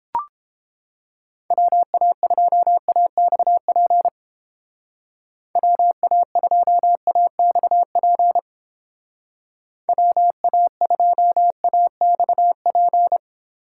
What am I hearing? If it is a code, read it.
Morse code, heard as a keyed tone, about 29 wpm, 700 Hz. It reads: WA2AXP WA2AXP WA2AXP